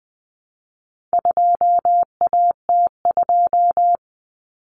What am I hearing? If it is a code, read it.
Morse code, heard as a keyed tone, about 20 wpm, 700 Hz. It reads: 2AT2
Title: Common Cut Numbers - 4-Digit 20wpm